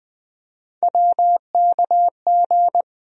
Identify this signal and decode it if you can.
Morse code, heard as a keyed tone, about 20 wpm, 700 Hz. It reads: WKG
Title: USA Call Sign Postfixes - 3 Letters 20wpm